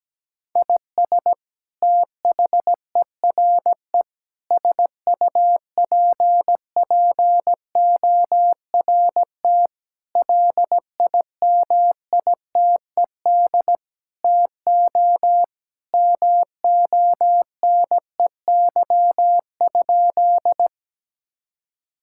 Morse code, heard as a keyed tone, about 17 wpm, 700 Hz. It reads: IS THERE SUPPORT LIMITED TO MONEY?
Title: Sentences from Top 1000 Words - Review Part 6 - 17wpm